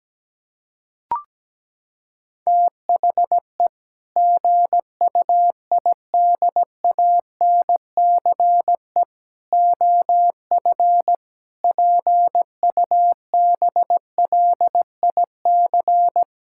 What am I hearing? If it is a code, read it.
Morse code, heard as a keyed tone, about 17 wpm, 700 Hz. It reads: THE GUIDANCE OF PUBLIC
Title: Sentences from Top 1600 Words 17wpm